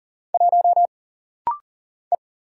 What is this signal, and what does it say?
Morse code, heard as a keyed tone, about 40 wpm, 700 Hz. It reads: 1 E
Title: Single Letter-Number Alt - Rapid-Fire 40wpm